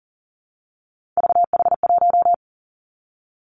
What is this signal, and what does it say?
Morse code, heard as a keyed tone, about 42 wpm, 700 Hz. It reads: VH1
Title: Three Challenging Letter-Number Combinations 40wpm